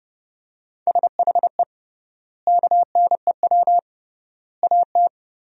Morse code, heard as a keyed tone, about 30 wpm, 700 Hz. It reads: SHE KNEW AT
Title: 3x Word-Spacing - Sentences from Top 400 Words - 30wpm